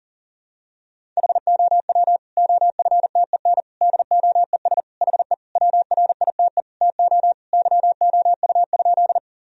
Morse code, heard as a keyed tone, about 37 wpm, 700 Hz. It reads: HOW OFTEN DOES HE WRITE TO YOU?
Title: Sentences from Top 1000 Words - Review Part 1 - 40wpm